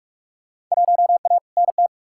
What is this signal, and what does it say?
Morse code, heard as a keyed tone, about 42 wpm, 700 Hz. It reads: 1A NT
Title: Field Day Exchanges Encore - 45wpm